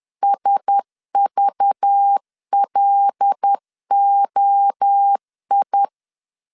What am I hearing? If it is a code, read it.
Morse code, heard as a keyed tone, about 10 wpm, 800 Hz. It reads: SVLOI
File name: LEZIONI / 6 PKX0 / 6 PKX 10wpm 2426